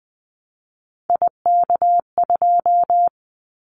Morse code, heard as a keyed tone, about 20 wpm, 700 Hz. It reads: IK2
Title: Callsign Prefixes for Top 25 DXCC - 20wpm